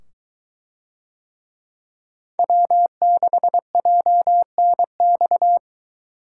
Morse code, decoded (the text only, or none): W6JNX